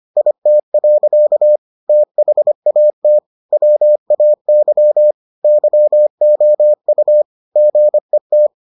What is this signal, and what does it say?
Morse code, heard as a keyed tone, about 25 wpm, 600 Hz. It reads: IT. THAT WAY YOU GET